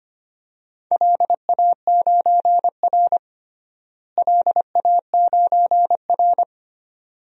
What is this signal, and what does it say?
Morse code, heard as a keyed tone, about 25 wpm, 700 Hz. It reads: LA9R LA9R
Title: DXCC-100 Call Signs - 4 Characters - Repeated 2x - 25wpm